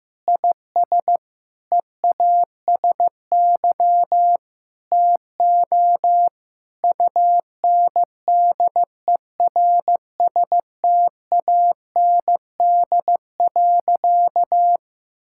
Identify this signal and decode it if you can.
Morse code, heard as a keyed tone, about 16 wpm, 700 Hz. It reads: IS EASY TO UNDERSTAND.
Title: Sentences from Top 1000 Words - Review Part 6 - 15wpm